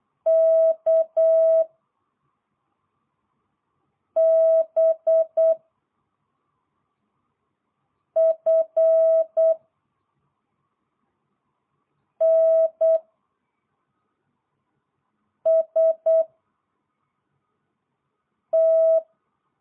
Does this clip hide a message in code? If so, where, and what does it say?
Yes: Morse code from 0.3 to 19.0 s, heard as a keyed tone, about 8 wpm, 660 Hz. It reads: K B F N S T